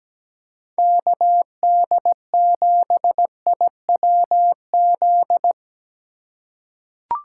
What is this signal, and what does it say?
Morse code, heard as a keyed tone, about 17 wpm, 700 Hz. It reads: KD7IWZ
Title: USA Call Signs - 6 Characters - 3x Character Spacing - 17wpm